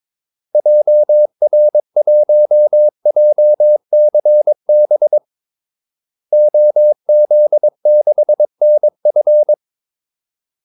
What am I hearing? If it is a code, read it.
Morse code, heard as a keyed tone, about 22 wpm, 600 Hz. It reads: JR1JCB OZ6NF
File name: Callsigns-22wpm